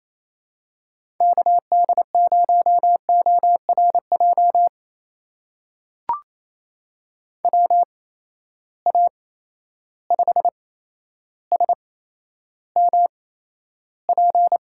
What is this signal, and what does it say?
Morse code, heard as a keyed tone, about 28 wpm, spaced as for 12 wpm, 700 Hz. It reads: KD0ORJ WA5SMP
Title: USA Call Signs - 6 Characters - 8x Character Spacing - 28wpm